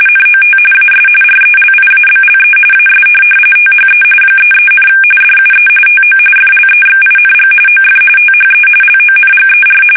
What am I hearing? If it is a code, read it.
RTTY, about 45.45 baud. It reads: NOT AS RELIABLE AS THE 33, DUE TO FLEXING OF THE LONGER
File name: fsk850 / teletype-corp